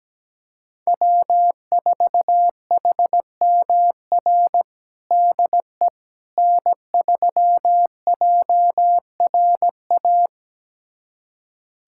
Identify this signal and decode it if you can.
Morse code, heard as a keyed tone, about 17 wpm, 700 Hz. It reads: W4HMR DE N3JRA
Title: USA Callsign DE Callsign - 5 Characters - 17wpm